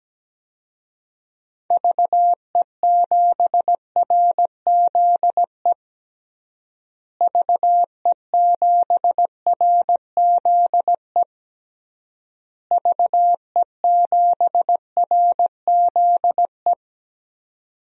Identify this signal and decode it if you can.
Morse code, heard as a keyed tone, about 17 wpm, 700 Hz. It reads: VE7RZE VE7RZE VE7RZE
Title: DXCC-100 Call Signs - 6 Characters - Repeated 3x - 17wpm